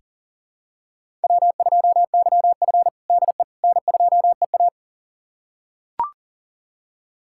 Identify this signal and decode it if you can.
Morse code, heard as a keyed tone, about 40 wpm, 700 Hz. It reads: W2YF DE N2EA
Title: USA Callsign DE Callsign - 4 Characters - 40wpm